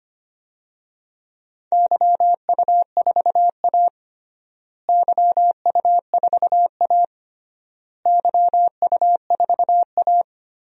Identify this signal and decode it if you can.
Morse code, heard as a keyed tone, about 25 wpm, 700 Hz. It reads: YU4A YU4A YU4A
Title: DXCC-100 Call Signs - 4 Characters - Repeated 3x - 25wpm